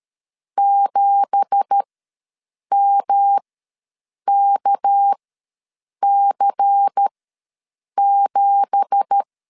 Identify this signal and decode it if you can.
Morse code, heard as a keyed tone, about 12 wpm, spaced as for 7 wpm, 800 Hz. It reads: 7MKC7
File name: LEZIONI / 9 2867 ws 12wpm 3431